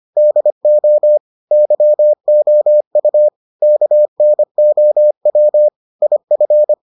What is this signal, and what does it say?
Morse code, heard as a keyed tone, about 25 wpm, 600 Hz. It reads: DO YOU KNOW IF